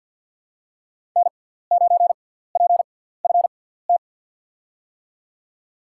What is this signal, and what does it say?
Morse code, heard as a keyed tone, about 53 wpm, spaced as for 17 wpm, 700 Hz. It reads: N9PFT
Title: USA Call Signs - 5 Characters - 6x Character Spacing - 50wpm